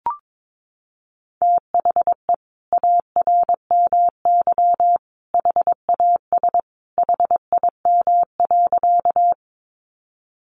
Morse code, heard as a keyed tone, about 22 wpm, 700 Hz. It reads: THE ARMY HAS HIM.
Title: Sentences from Top 800 Words 22wpm